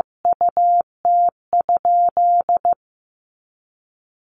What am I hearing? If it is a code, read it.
Morse code, heard as a keyed tone, about 15 wpm, 700 Hz. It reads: UT?
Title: Sentences from Top 100 Words 15wpm